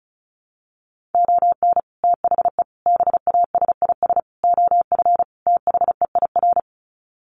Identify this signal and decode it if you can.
Morse code, heard as a keyed tone, about 34 wpm, 700 Hz. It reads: ON THE BASIS OF THEIR
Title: Sets of 5 Words Encore 35wpm